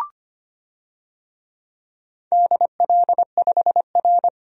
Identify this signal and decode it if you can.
Morse code, heard as a keyed tone, about 25 wpm, 700 Hz. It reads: DL5R
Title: DXCC-100 Call Signs - 4 Characters - 25wpm